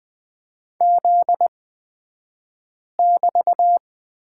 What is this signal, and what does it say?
Morse code, heard as a keyed tone, about 20 wpm, spaced as for 5 wpm, 700 Hz. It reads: Z<BT>
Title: 242 Lesson 39d - Letter z - Two Characters 20wpm